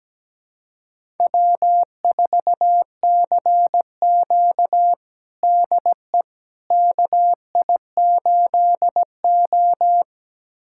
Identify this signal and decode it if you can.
Morse code, heard as a keyed tone, about 17 wpm, 700 Hz. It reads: W4CQ DE KI8O
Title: USA Callsign DE Callsign - 4 Characters - 17wpm